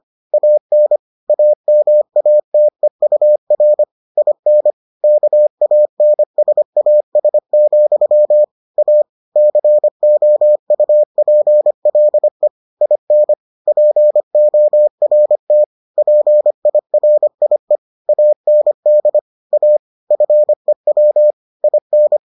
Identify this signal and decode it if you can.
Morse code, heard as a keyed tone, about 25 wpm, 600 Hz. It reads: AN AMATEUR IN KANSAS, A COUPLE IN PORT PIRIE AND A FEW IN